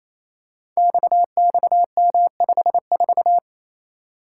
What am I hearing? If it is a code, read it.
Morse code, heard as a keyed tone, about 28 wpm, 700 Hz. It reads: XXM54
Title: Code Groups - 5 Letter:Numbers - 28wpm